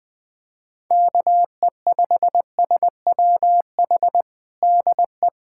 Morse code, heard as a keyed tone, about 20 wpm, 700 Hz. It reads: KE5SWH DE
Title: USA Callsign DE Callsign - 6 Characters - 20wpm